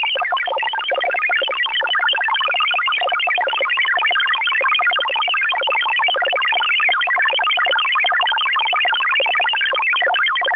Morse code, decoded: none